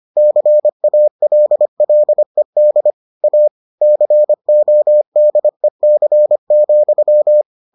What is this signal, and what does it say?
Morse code, heard as a keyed tone, about 25 wpm, 600 Hz. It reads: CALLED A CODEC,